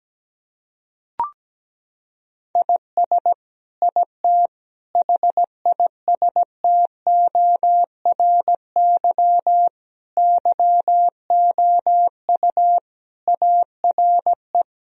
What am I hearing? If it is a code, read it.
Morse code, heard as a keyed tone, about 17 wpm, 700 Hz. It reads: IS IT HISTORY YOU ARE
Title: Sentences from Top 1000 Words - Review Part 3 - 17wpm